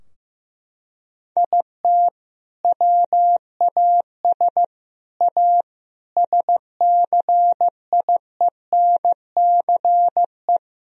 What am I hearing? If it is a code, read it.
Morse code, heard as a keyed tone, about 15 wpm, 700 Hz. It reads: IT WAS A SCIENCE
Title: Sentences from Top 1000 Words - Review Part 2 - 15wpm